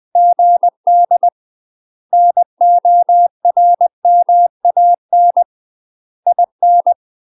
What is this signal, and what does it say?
Morse code, heard as a keyed tone, about 20 wpm, 700 Hz. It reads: GD NORMAN IN
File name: UK MASTER QSOs TRAINING / 20WPM / UK MASTER QSOs 0066